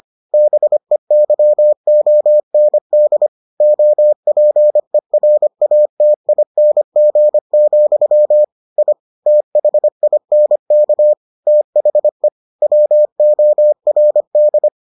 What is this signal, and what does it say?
Morse code, heard as a keyed tone, about 25 wpm, 600 Hz. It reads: BEYOND OPERATING, I THINK THE WORD